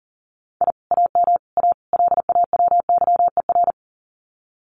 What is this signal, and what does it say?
Morse code, heard as a keyed tone, about 42 wpm, 700 Hz. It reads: I AM A LAWYER
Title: Common QSO Phrases Encore 40wpm